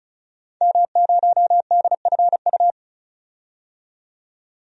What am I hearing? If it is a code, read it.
Morse code, heard as a keyed tone, about 36 wpm, 700 Hz. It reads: M0DFU
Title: DXCC-100 Call Signs - 5 Characters - 35wpm